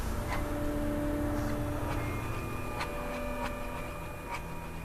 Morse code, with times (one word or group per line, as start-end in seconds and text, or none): none